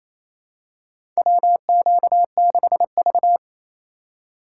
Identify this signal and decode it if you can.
Morse code, heard as a keyed tone, about 28 wpm, 700 Hz. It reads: WQ6V
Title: USA Call Signs - 4 Characters - 2x Character Spacing - 28wpm